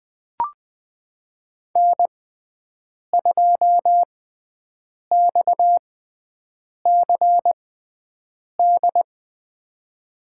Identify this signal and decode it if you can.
Morse code, heard as a keyed tone, about 21 wpm, spaced as for 7 wpm, 700 Hz. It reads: N2XCD
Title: USA Call Signs - 6x Character Spacing - 20wpm